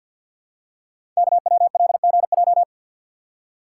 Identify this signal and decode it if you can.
Morse code, heard as a keyed tone, about 47 wpm, 700 Hz. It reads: KWLGJ
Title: Code Groups - 5 Letters - 50wpm